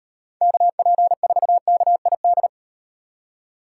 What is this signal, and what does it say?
Morse code, heard as a keyed tone, about 38 wpm, 700 Hz. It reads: KP4KID